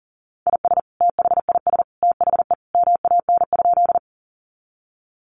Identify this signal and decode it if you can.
Morse code, heard as a keyed tone, about 40 wpm, 700 Hz. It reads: IS THIS THE MAN?
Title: Sentences from Top 200 Words 40wpm